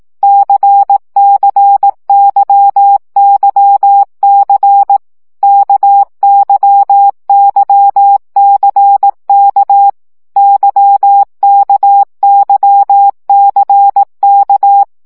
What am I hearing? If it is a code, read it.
Morse code, heard as a keyed tone, about 18 wpm, 800 Hz. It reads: CCYYC KYYCK YKYCK
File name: CW gyakorlas k c y 15 baud